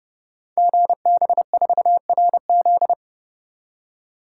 Morse code, heard as a keyed tone, about 30 wpm, 690 Hz. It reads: GB4RZ
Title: Code Groups - 5 Letter:Numbers - 30wpm